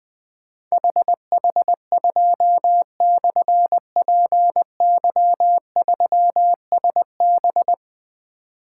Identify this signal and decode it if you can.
Morse code, heard as a keyed tone, about 20 wpm, 700 Hz. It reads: HH2/PY3SB
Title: Difficult Call Signs 20wpm